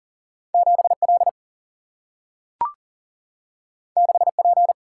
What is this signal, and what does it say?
Morse code, heard as a keyed tone, about 40 wpm, 700 Hz. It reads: 7L 6P
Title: Single-Letter Single-Number 40wpm